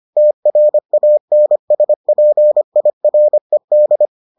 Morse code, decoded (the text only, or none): TRANSPIRED